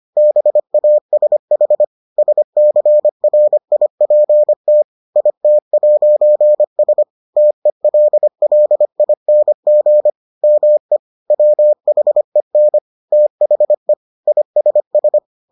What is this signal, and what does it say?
Morse code, heard as a keyed tone, about 25 wpm, 600 Hz. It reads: BASH SCRIPT IT'S TELLING ME WHEN THE ISS